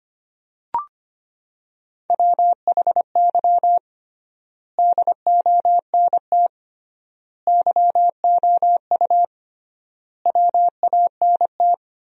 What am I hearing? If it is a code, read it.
Morse code, heard as a keyed tone, about 25 wpm, 700 Hz. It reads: WHY DONT YOU WANT
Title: 3x Word-Spacing - Sentences from Top 300 Words - 25wpm